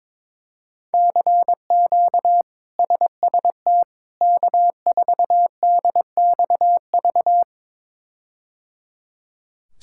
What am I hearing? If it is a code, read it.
Morse code, heard as a keyed tone, about 22 wpm, 700 Hz. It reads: CQ SST K4DXV
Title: K1USN SST QSO Practice - 22wpm